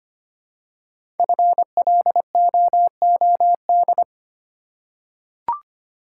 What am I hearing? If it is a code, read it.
Morse code, heard as a keyed tone, about 25 wpm, 700 Hz. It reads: FLOOD
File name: 5 Letter Words - First 3 Letters Given - 25wpm